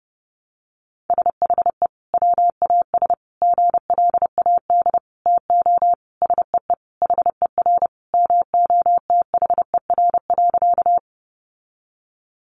Morse code, heard as a keyed tone, about 30 wpm, 700 Hz. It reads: SHE WAS GLAD TO SEE HER MOTHER.
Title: Sentences from Top 800 Words 30wpm